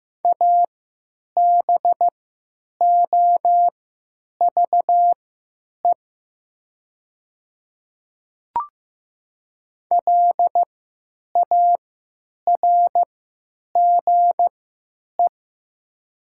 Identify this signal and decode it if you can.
Morse code, heard as a keyed tone, about 15 wpm, spaced as for 9 wpm, 700 Hz. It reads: ABOVE LARGE
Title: 3x Character Spacing - Top 500 Words - 15wpm